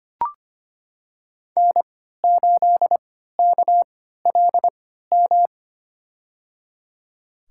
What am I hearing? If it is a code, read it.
Morse code, heard as a keyed tone, about 25 wpm, spaced as for 14 wpm, 700 Hz. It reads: N8KLM
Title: USA Call Signs - 5 Characters - 3x Character Spacing - 25wpm